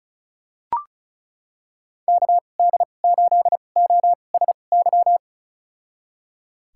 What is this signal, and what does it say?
Morse code, heard as a keyed tone, about 36 wpm, spaced as for 25 wpm, 700 Hz. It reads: KD8OSY
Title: USA Call Signs - 6 Characters - 2x Character Spacing - 35wpm